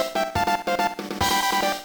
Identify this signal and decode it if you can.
There is no code.